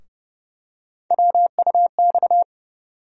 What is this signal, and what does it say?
Morse code, heard as a keyed tone, about 30 wpm, 700 Hz. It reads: WUX
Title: USA Call Sign Postfixes - 3 Letters 30wpm